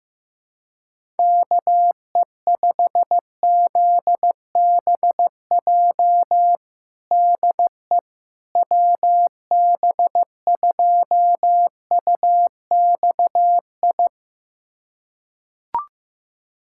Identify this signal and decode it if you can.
Morse code, heard as a keyed tone, about 15 wpm, 700 Hz. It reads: KE5ZBJ DE WB2UXI
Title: USA Callsign DE Callsign 15wpm